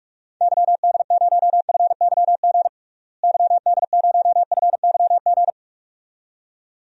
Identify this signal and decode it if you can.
Morse code, heard as a keyed tone, about 47 wpm, 700 Hz. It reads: YD0FYG YD0FYG
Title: DXCC-100 Call Signs - 6 Characters - Repeated 2x - 45wpm